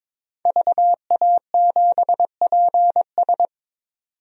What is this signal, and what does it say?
Morse code, heard as a keyed tone, about 22 wpm, 700 Hz. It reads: VA7PS
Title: DXCC-100 Call Signs - 5 Characters - 22wpm